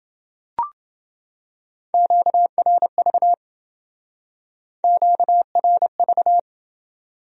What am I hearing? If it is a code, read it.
Morse code, heard as a keyed tone, about 28 wpm, 700 Hz. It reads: QRV QRV